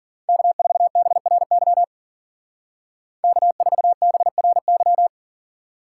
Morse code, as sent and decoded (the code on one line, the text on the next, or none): -.- ....- -... .-. -.-- / -.- ....- -... .-. -.--
K4BRY K4BRY